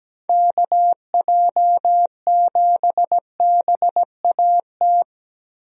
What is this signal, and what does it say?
Morse code, heard as a keyed tone, about 17 wpm, 700 Hz. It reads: KJ7BAT
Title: USA Call Signs - 6 Characters - Repeated 3x - 17wpm